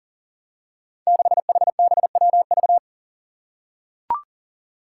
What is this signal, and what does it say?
Morse code, heard as a keyed tone, about 38 wpm, 700 Hz. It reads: 6HBWV
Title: Five Challenging Letter-Number Combinations 40wpm